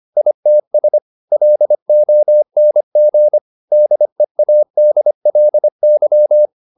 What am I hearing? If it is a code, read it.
Morse code, heard as a keyed tone, about 25 wpm, 600 Hz. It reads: ITS LONG DEADLY
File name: the old man and the sea part 1 25wpm